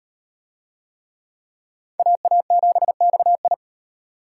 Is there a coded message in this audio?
Yes: Morse code from 2.0 to 3.5 s, heard as a keyed tone, about 34 wpm, 700 Hz. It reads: AA7XI